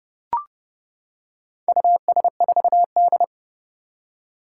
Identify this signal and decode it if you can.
Morse code, heard as a keyed tone, about 30 wpm, 700 Hz. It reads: US4D
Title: Four Challenging Letter-Number Combinations 30wpm